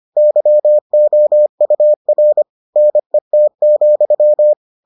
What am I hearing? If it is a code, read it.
Morse code, heard as a keyed tone, about 25 wpm, 600 Hz. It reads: YOUR NET,